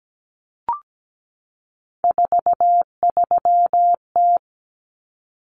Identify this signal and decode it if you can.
Morse code, heard as a keyed tone, about 17 wpm, 690 Hz. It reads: 43T
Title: Common Cut Numbers - 3-Digit 17wpm